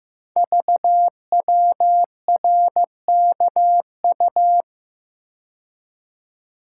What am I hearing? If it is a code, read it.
Morse code, heard as a keyed tone, about 15 wpm, 700 Hz. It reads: VWRKU
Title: Code Groups - 5 Letter:Numbers - 15wpm